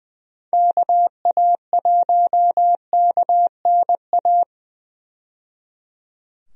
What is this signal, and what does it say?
Morse code, heard as a keyed tone, about 20 wpm, 700 Hz. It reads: KA1KNA
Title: USA Call Signs - 6 Characters - 20wpm